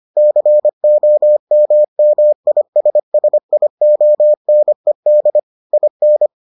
Morse code, heard as a keyed tone, about 25 wpm, 600 Hz. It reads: COMMISSIONED IN